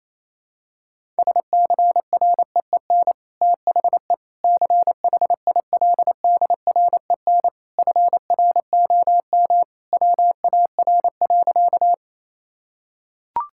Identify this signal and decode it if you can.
Morse code, heard as a keyed tone, about 28 wpm, 700 Hz. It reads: SCREEN THE CHILDREN FROM WAR.
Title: Sentences from Top 1000 Words - Review Part 6 - 28wpm